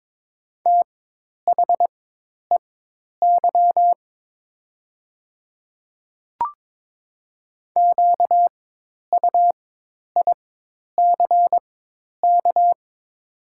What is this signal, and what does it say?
Morse code, heard as a keyed tone, about 22 wpm, spaced as for 10 wpm, 700 Hz. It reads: THEY QUICK
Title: 4x Character Spacing - Top 500 Words - 22wpm